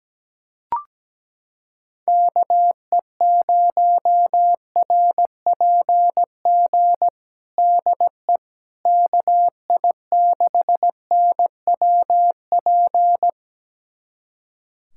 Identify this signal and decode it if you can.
Morse code, heard as a keyed tone, about 17 wpm, 700 Hz. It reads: KE0RPG DE KI6NWP
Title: USA Callsign DE Callsign - 6 Characters - 17wpm